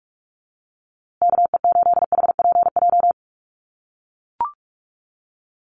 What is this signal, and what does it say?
Morse code, heard as a keyed tone, about 44 wpm, 700 Hz. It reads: KE8HPJ
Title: USA Call Signs Repeated 3x 45wpm